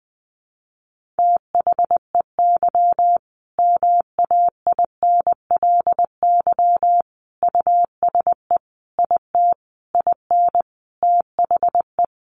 Morse code, heard as a keyed tone, about 20 wpm, 700 Hz. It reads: THEY MAINLY USE IT IN THE
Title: Sentences from Top 1400 Words 20wpm